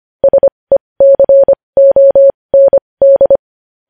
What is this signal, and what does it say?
Morse code, heard as a keyed tone, about 25 wpm, spaced as for 20 wpm, 550 Hz. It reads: SECOND
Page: news-headlines-in-morse-code-at-20-wpm